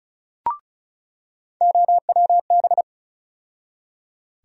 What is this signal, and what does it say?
Morse code, heard as a keyed tone, about 36 wpm, 700 Hz. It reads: OWB
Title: USA Call Sign Postfixes - 3 Letters 35wpm